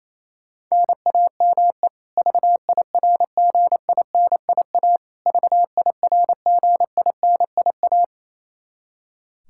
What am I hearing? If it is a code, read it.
Morse code, heard as a keyed tone, about 28 wpm, 700 Hz. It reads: NAME VIRGINIA VIRGINIA
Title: Practice QSO - Exchange 1 - Beginning - 28wpm